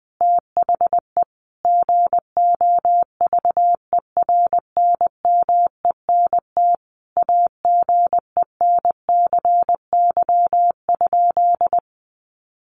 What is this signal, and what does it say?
Morse code, heard as a keyed tone, about 20 wpm, 700 Hz. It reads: THE GOVERNMENT AGENCY?
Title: Sentences from Top 1300 Words 20wpm